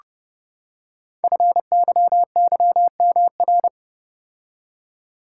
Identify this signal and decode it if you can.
Morse code, heard as a keyed tone, about 30 wpm, 700 Hz. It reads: FYYMR
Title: Code Groups - 5 Letter:Numbers - 30wpm